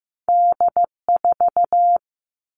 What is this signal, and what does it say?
Morse code, heard as a keyed tone, about 15 wpm, 700 Hz. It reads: D4
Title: Two Challenging Letter-Number Combinations 15wpm